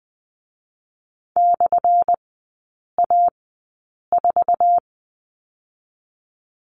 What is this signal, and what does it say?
Morse code, heard as a keyed tone, about 20 wpm, 700 Hz. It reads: / A 4
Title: 205 Lesson 33e - Letter j - Three Characters 20wpm